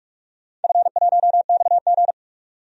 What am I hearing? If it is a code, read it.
Morse code, heard as a keyed tone, about 42 wpm, 700 Hz. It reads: F1XG
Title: DXCC-100 Call Signs - 4 Characters - 45wpm